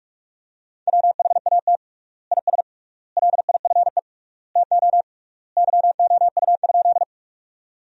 Morse code, heard as a keyed tone, about 46 wpm, 700 Hz. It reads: WHAT IS LIFE TO YOU?
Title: 3x Word-Spacing - Sentences from Top 300 Words - 45wpm